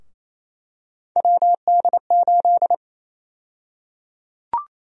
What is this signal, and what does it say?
Morse code, heard as a keyed tone, about 28 wpm, 700 Hz. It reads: WD8
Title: USA Call Sign Prefixes - 28wpm